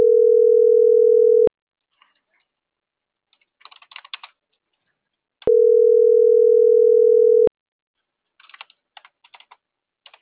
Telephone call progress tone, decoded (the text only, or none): ringback tone